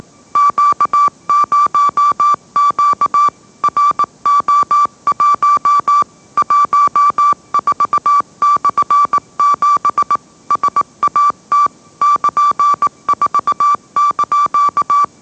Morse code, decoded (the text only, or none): Q0Q RO1 14/7 SAT <KN>4)